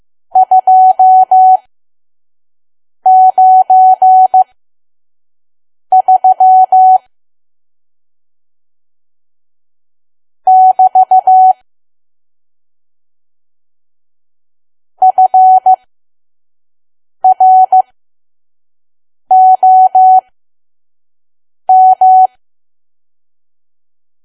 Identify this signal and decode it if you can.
Morse code, heard as a keyed tone, about 15 wpm, spaced as for 5 wpm, 750 Hz. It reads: 293 <BT> FROM
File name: Archive / 5 WPM files / PM